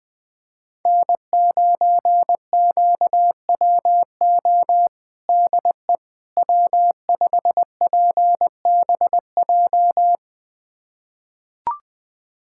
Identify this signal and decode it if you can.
Morse code, heard as a keyed tone, about 20 wpm, 700 Hz. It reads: N9QWO DE W5PBJ
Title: USA Callsign DE Callsign - 5 Characters - 20wpm